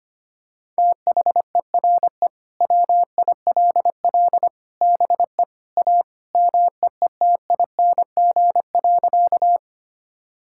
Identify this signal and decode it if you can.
Morse code, heard as a keyed tone, about 25 wpm, 700 Hz. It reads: THERE WILL BE A MEETING.
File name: practice / 25wpm / Sentences with 3-Word Phrases - 25wpm